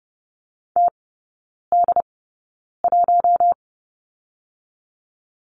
Morse code, heard as a keyed tone, about 30 wpm, 700 Hz. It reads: T D 1
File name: Lesson 19e - Letter w - Three Characters 30wpm